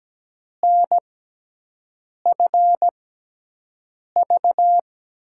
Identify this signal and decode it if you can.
Morse code, heard as a keyed tone, about 17 wpm, spaced as for 6 wpm, 700 Hz. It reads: NFV
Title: USA Call Signs - 5 Characters - 6x Character Spacing - 17wpm